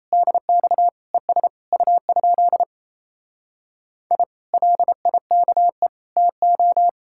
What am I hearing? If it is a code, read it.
Morse code, heard as a keyed tone, about 38 wpm, 700 Hz. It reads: DX ES U? I LIKE TO